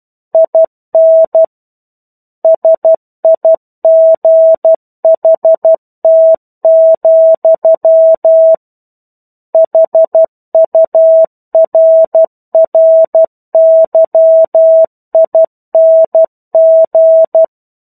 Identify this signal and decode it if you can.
Morse code, heard as a keyed tone, about 12 wpm, 650 Hz. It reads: IN SIGHT, HURRYING
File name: 01-Chapter-01 12 WPM